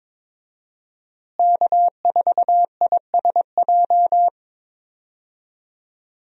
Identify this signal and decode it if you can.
Morse code, heard as a keyed tone, about 22 wpm, 700 Hz. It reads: K4ISJ
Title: USA Call Signs - 5 Characters - Repeated 2x - 22wpm